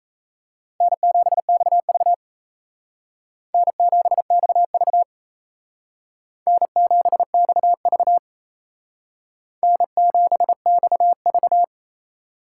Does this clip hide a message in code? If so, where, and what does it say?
Yes: Morse code from 0.8 to 11.6 s, heard as a keyed tone, about 36 wpm, 700 Hz. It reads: N7XV N7XV N7XV N7XV